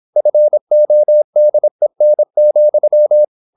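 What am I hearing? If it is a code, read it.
Morse code, heard as a keyed tone, about 26 wpm, 600 Hz. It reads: FODEN,
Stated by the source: High-speed (QRQ) Morse at 26 wpm.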